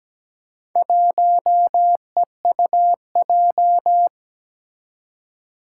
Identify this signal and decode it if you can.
Morse code, heard as a keyed tone, about 17 wpm, 700 Hz. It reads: 1EUJ
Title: Four Challenging Letter-Number Combinations 17wpm